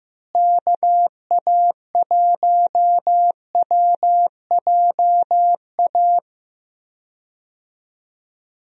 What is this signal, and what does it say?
Morse code, heard as a keyed tone, about 15 wpm, 700 Hz. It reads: KA1WJA
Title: USA Call Signs - 6 Characters - Repeated 3x - 15wpm